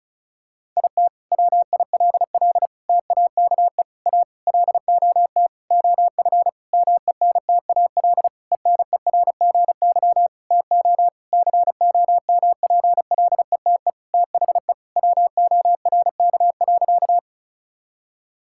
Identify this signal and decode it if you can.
Morse code, heard as a keyed tone, about 36 wpm, 700 Hz. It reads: IT WILL TAKE A LOT OF MENTAL ENERGY TO COMPLETE THE WORK.
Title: Sentences from Top 1000 Words - Review Part 6 - 35wpm